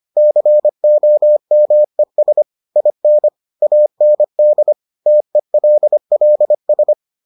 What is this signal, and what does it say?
Morse code, heard as a keyed tone, about 25 wpm, 600 Hz. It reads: COMES IN AND TELLS